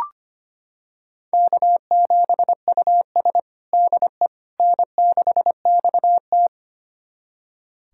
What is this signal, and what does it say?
Morse code, heard as a keyed tone, about 25 wpm, 700 Hz. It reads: K7US DE N6XT
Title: USA Callsign DE Callsign - 4 Characters - 25wpm